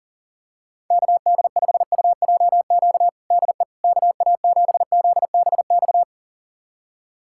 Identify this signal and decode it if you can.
Morse code, heard as a keyed tone, about 40 wpm, 700 Hz. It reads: KD5UJQ DE KA7ZBX
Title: USA Callsign DE Callsign - 6 Characters - 40wpm